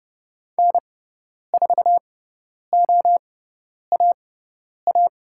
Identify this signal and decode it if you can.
Morse code, heard as a keyed tone, about 30 wpm, spaced as for 10 wpm, 700 Hz. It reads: N4OAA
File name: Lesson 17g - Letter u - Call Signs 30wpm